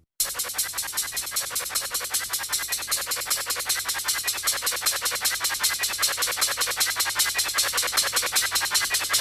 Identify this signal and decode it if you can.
Emergency Alert System (SAME) tones (header bursts), heard as harsh buzzing bursts. It is heard from 0.6 to 8.9 s.